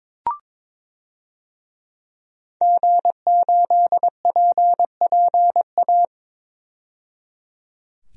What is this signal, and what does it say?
Morse code, heard as a keyed tone, about 22 wpm, 700 Hz. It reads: G8PPA
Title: DXCC-100 Call Signs - 5 Characters - 22wpm